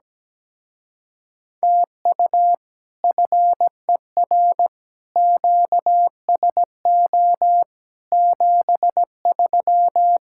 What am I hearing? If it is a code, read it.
Morse code, heard as a keyed tone, about 17 wpm, 700 Hz. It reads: TU FER QSO 73
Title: Speed-Racing Common QSO Phrases Encore - 17wpm